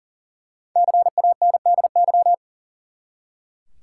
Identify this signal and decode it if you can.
Morse code, heard as a keyed tone, about 40 wpm, 700 Hz. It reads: CANDY
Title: Five Letter Words - 40wpm